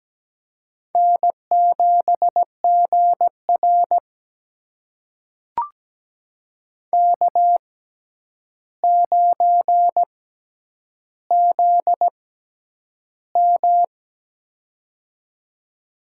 Morse code, heard as a keyed tone, about 17 wpm, spaced as for 9 wpm, 700 Hz. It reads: N7GR K9ZM
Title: USA Call Signs - 4 Characters - 6x Character Spacing - 17wpm